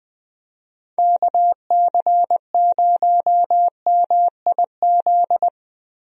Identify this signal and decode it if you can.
Morse code, heard as a keyed tone, about 20 wpm, 700 Hz. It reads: KC0MIZ
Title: USA Call Signs - 6 Characters - Repeated 2x - 20wpm